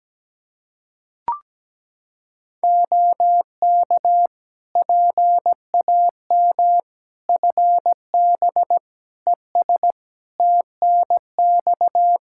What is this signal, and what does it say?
Morse code, heard as a keyed tone, about 17 wpm, 700 Hz. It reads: OK PAM FB ES TNX
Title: K1USN SST QSO Practice 17wpm